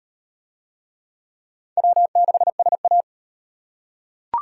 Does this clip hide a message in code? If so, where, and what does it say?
Yes: Morse code from 1.8 to 3.0 s, heard as a keyed tone, about 35 wpm, 700 Hz. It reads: W6SA